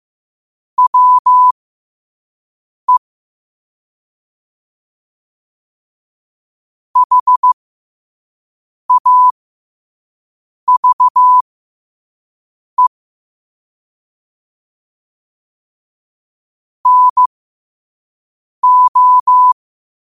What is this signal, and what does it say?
Morse code, heard as a keyed tone, about 15 wpm, spaced as for 5 wpm, 1000 Hz. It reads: WE HAVE NO
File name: qotdfiles / 5wpm